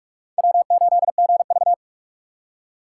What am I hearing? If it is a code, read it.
Morse code, heard as a keyed tone, about 38 wpm, 700 Hz. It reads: W8GV